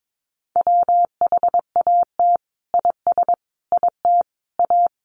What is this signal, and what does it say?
Morse code, heard as a keyed tone, about 22 wpm, 700 Hz. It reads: WHAT IS IT A
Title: Sentences from Top 1300 Words 22wpm